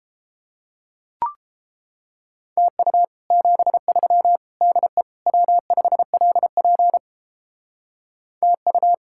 Morse code, heard as a keyed tone, about 29 wpm, 700 Hz. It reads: TU 73 DE W5LP TU